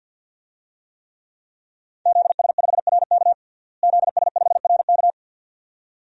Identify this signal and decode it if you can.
Morse code, heard as a keyed tone, about 51 wpm, 700 Hz. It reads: ZS5RK ZS5RK
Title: DXCC-100 Call Signs - 5 Characters - Repeated 2x - 50wpm